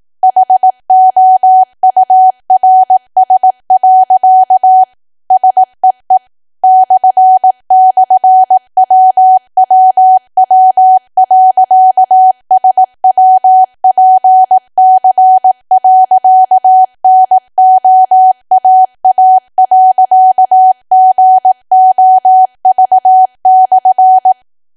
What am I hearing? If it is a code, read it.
Morse code, heard as a keyed tone, about 18 wpm, 750 Hz. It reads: HOURS. SEE //WWW.SWPC.NOAA.GOV/
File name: cwa / W1AW-Level2 / 18WPM